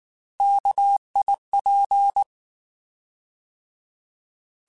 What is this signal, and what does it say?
Morse code, heard as a keyed tone, about 19 wpm, 800 Hz. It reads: KIP